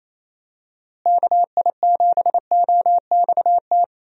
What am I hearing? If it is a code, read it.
Morse code, heard as a keyed tone, about 27 wpm, 700 Hz. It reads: KI7OXT
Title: USA Call Signs - 6 Characters - Repeated 2x - 28wpm